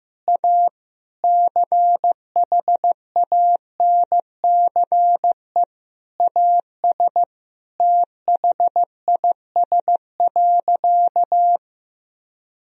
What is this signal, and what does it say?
Morse code, heard as a keyed tone, about 15 wpm, 700 Hz. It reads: A CHANCE AS THIS.
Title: Sentences from Top 1000 Words - Review Part 1 - 15wpm